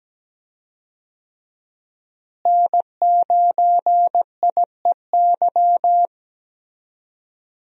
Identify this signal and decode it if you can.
Morse code, heard as a keyed tone, about 17 wpm, 700 Hz. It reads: N9IEY
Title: USA Call Signs - 5 Characters - Repeated 2x - 17wpm